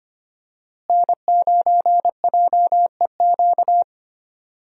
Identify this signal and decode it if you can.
Morse code, heard as a keyed tone, about 25 wpm, 700 Hz. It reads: N9JEQ
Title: USA Call Signs - 5 Characters - Repeated 3x - 25wpm